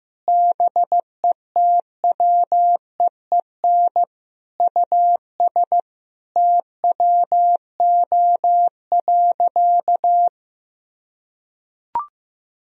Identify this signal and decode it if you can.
Morse code, heard as a keyed tone, about 15 wpm, 700 Hz. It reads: BETWEEN US TWO.
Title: Sentences from Top 1000 Words - Review Part 1 - 15wpm